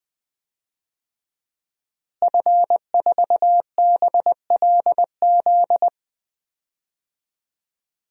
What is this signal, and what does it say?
Morse code, heard as a keyed tone, about 20 wpm, 700 Hz. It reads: F4BLZ
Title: DXCC-100 Call Signs - 5 Characters - Repeated 3x - 20wpm